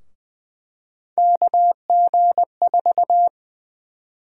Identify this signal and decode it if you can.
Morse code, heard as a keyed tone, about 20 wpm, 700 Hz. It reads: KG4
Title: USA Call Sign Prefixes - Repeated 2x - 20wpm